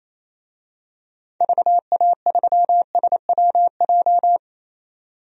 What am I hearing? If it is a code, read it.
Morse code, heard as a keyed tone, about 28 wpm, 700 Hz. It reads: VA3SWJ
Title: DXCC-100 Call Signs - 6 Characters - 28wpm